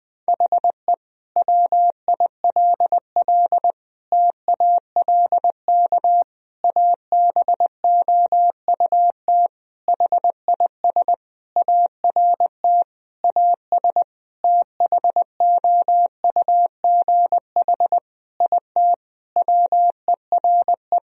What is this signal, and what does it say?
Morse code, heard as a keyed tone, about 20 wpm, 700 Hz. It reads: HE WILL TALK ABOUT HIS ART AS THOUGH IT WERE
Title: Sentences from Top 1000 Words - Review Part 3 - 20wpm